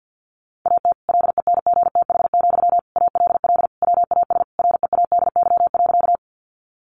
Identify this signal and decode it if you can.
Morse code, heard as a keyed tone, about 49 wpm, 700 Hz. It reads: AT LENGTH, ALL WAS READY.
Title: Sentences from Top 1000 Words 50wpm